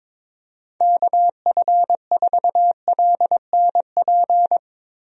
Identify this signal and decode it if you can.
Morse code, heard as a keyed tone, about 22 wpm, 700 Hz. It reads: KF4LNP
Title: USA Call Signs - 6 Characters - Repeated 3x - 22wpm